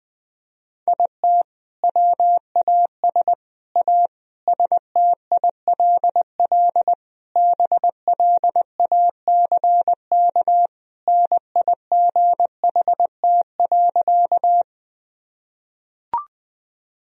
Morse code, heard as a keyed tone, about 20 wpm, 700 Hz. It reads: IT WAS A STILL BLACK NIGHT.
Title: Sentences from Top 1000 Words - Review Part 2 - 20wpm